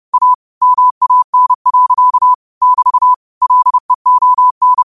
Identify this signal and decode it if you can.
Morse code, heard as a keyed tone, about 30 wpm, 1000 Hz. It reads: A MAN. <BT> LEON